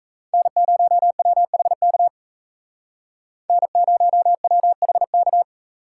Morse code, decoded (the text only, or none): N0WHK N0WHK